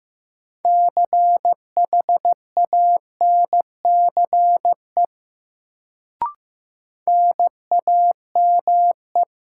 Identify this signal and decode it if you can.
Morse code, heard as a keyed tone, about 15 wpm, 700 Hz. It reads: CHANCE NAME
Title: Sets of 3 Words - New Word Drill - 15wpm